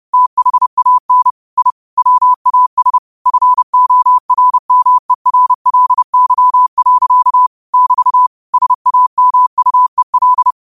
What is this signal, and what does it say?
Morse code, heard as a keyed tone, about 30 wpm, 1000 Hz. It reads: THAN I WAS FORMERLY. <BT> SAMUEL